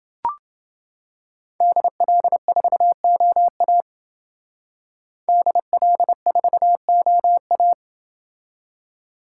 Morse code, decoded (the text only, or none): DL4OA DL4OA